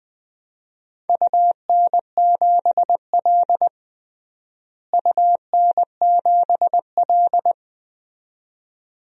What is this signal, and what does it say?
Morse code, heard as a keyed tone, about 20 wpm, 700 Hz. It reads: UN7L UN7L
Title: DXCC-100 Call Signs - 4 Characters - Repeated 2x - 20wpm